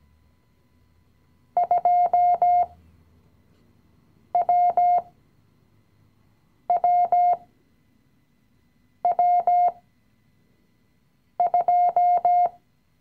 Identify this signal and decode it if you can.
Morse code, heard as a keyed tone, about 17 wpm, spaced as for 5 wpm, 710 Hz. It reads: 2WWW2
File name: CW-Online-Kurs / Anfänger / Lektion 017